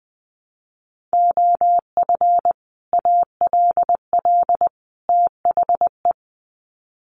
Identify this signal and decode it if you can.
Morse code, heard as a keyed tone, about 20 wpm, 700 Hz. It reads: OF ALL THE
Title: Sets of 3 Words 20wpm